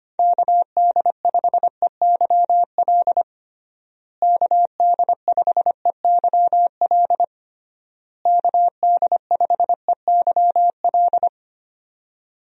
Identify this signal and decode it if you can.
Morse code, heard as a keyed tone, about 25 wpm, 700 Hz. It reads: KD5EYL KD5EYL KD5EYL
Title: USA Call Signs - 6 Characters - Repeated 3x - 25wpm